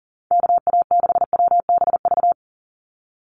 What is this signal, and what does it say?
Morse code, heard as a keyed tone, about 41 wpm, 700 Hz. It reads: KA6WBV
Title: USA Call Signs 40wpm